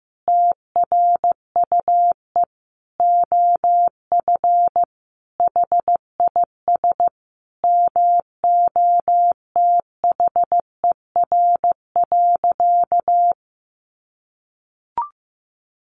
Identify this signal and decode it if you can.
Morse code, heard as a keyed tone, about 15 wpm, 700 Hz. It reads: TRUE OF HIS MOTHER.
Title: Sentences from Top 700 Words 15wpm